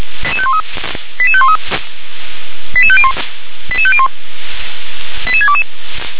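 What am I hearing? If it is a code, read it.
Five-tone selective call (ZVEI), heard as quick runs of five tones, tones of about 70 ms. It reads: 88513 88501 88501 88622
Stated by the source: ZVEI five-tone selective call 88731 received by SDR.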